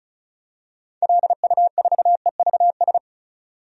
Morse code, heard as a keyed tone, about 34 wpm, 700 Hz. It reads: LU4EVS
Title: DXCC-100 Call Signs - 6 Characters - 35wpm